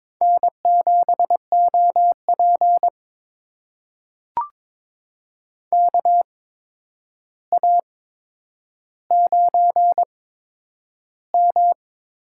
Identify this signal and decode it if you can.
Morse code, heard as a keyed tone, about 22 wpm, spaced as for 9 wpm, 700 Hz. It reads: N7OP KA9M
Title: USA Call Signs - 4 Characters - 8x Character Spacing - 22wpm